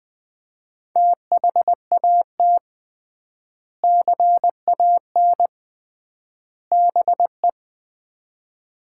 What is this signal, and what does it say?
Morse code, heard as a keyed tone, about 20 wpm, 700 Hz. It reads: THAT CAN BE
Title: 3x Word-Spacing - Sets of 3 Words - 20wpm